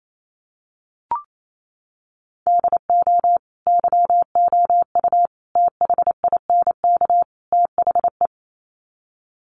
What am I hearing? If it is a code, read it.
Morse code, heard as a keyed tone, about 28 wpm, 700 Hz. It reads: DO YOU THINK THE
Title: Sets of 4 Words 28wpm